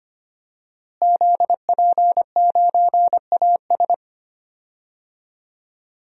Morse code, heard as a keyed tone, about 25 wpm, 700 Hz. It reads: ZP9AS
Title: DXCC-100 Call Signs - 5 Characters - Repeated 3x - 25wpm